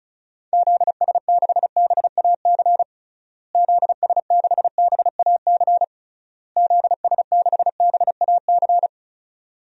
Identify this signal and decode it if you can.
Morse code, heard as a keyed tone, about 36 wpm, 700 Hz. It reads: ZS6BAC ZS6BAC ZS6BAC
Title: DXCC-100 Call Signs - 6 Characters - Repeated 3x - 35wpm